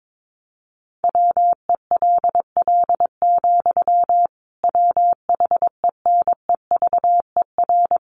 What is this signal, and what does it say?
Morse code, heard as a keyed tone, about 22 wpm, 700 Hz. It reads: WELL, WHENEVER
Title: Sentences from Top 1200 Words 22wpm